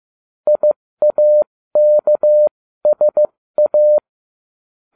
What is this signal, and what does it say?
Morse code, heard as a keyed tone, about 15 wpm, spaced as for 13 wpm, 600 Hz. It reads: IAKSA